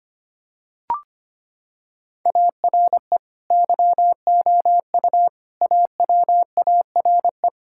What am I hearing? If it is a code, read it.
Morse code, heard as a keyed tone, about 25 wpm, 700 Hz. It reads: ARE YOU AWARE
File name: practice / 25wpm / Sentences with 2-Word Phrases - 25wpm